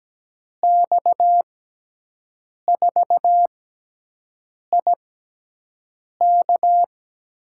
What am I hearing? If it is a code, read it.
Morse code, heard as a keyed tone, about 17 wpm, spaced as for 6 wpm, 700 Hz. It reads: X4IK
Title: USA Call Signs - 5 Characters - 6x Character Spacing - 17wpm